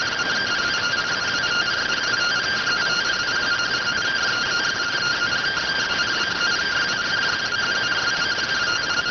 Two-tone teleprinter signal, about 170 Hz shift.